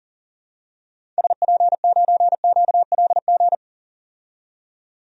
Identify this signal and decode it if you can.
Morse code, heard as a keyed tone, about 40 wpm, 700 Hz. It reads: SP9QLG
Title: DXCC-100 Call Signs - 6 Characters - 40wpm